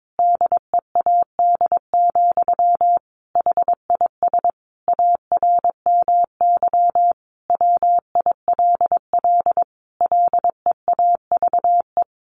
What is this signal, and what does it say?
Morse code, heard as a keyed tone, about 22 wpm, 700 Hz. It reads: DEAD, HIS ARMY WILL LEAVE
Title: Sentences from Top 1500 Words 22wpm